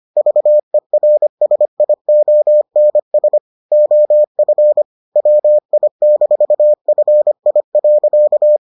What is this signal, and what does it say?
Morse code, heard as a keyed tone, about 25 wpm, 600 Hz. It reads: VERSIONS OF WI-FI.